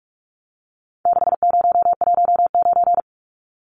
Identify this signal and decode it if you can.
Morse code, heard as a keyed tone, about 44 wpm, 700 Hz. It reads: 6019
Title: Numbers - 4-Digit 45wpm